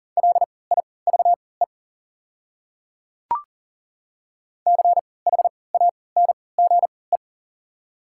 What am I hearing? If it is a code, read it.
Morse code, heard as a keyed tone, about 40 wpm, spaced as for 23 wpm, 690 Hz. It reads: LIVE CHANGE
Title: 3x Character Spacing - Top 200 Words - 40wpm